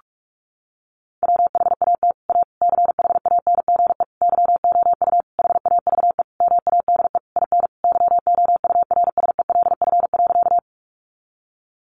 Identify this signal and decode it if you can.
Morse code, heard as a keyed tone, about 44 wpm, 700 Hz. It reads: WHAT A CHANGE YOU HAVE MADE IN YOURSELF.
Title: Sentences from Top 600 Words 45wpm